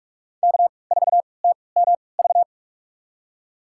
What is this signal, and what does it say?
Morse code, heard as a keyed tone, about 46 wpm, spaced as for 26 wpm, 700 Hz. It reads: K4TMV
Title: USA Call Signs - 5 Characters - 3x Character Spacing - 45wpm